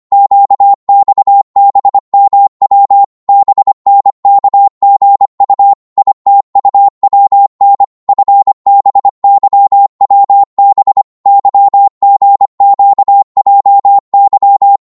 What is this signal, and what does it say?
Morse code, heard as a keyed tone, about 25 wpm, 800 Hz. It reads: QXBMW BNKGU ITUWN FBYWB YGQJY